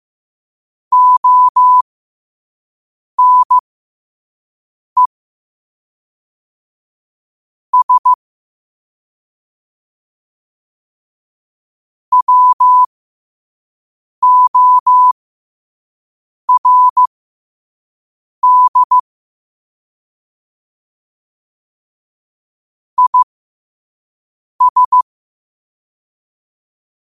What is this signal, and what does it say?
Morse code, heard as a keyed tone, about 15 wpm, spaced as for 5 wpm, 1000 Hz. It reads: ONE S WORD IS
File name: qotdfiles / 5wpm